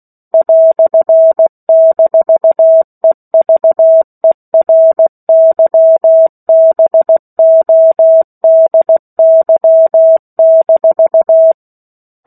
Morse code, heard as a keyed tone, about 16 wpm, 650 Hz. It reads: "-EVERYBODY-
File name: 03-Chapter-03 16 WPM